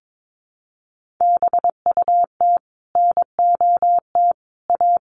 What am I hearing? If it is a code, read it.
Morse code, heard as a keyed tone, about 22 wpm, 700 Hz. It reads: BUT NOT A
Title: Sentences from Top 1400 Words 22wpm